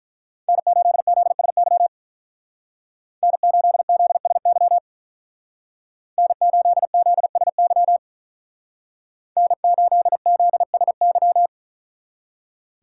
Morse code, heard as a keyed tone, about 43 wpm, 700 Hz. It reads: N8ZSY N8ZSY N8ZSY N8ZSY
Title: Speed Racing - USA 1x3 Call Signs 35wpm